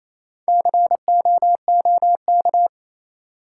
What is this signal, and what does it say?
Morse code, heard as a keyed tone, about 28 wpm, 700 Hz. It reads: COOK
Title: Sentences from Top 700 Words - New Word Drill - 28wpm